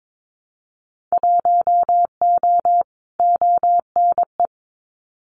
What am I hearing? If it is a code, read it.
Morse code, heard as a keyed tone, about 22 wpm, 700 Hz. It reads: 1O ONE
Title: Winter Field Day Exchanges 22wpm